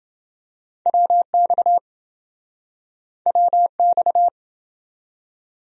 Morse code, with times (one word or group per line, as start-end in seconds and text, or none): 0.9-1.8 WX
3.3-4.3 WX